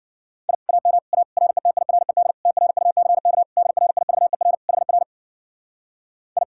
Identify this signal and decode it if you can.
Morse code, heard as a keyed tone, about 50 wpm, 700 Hz. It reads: I AM A RETIRED TRUCK DRIVER HR I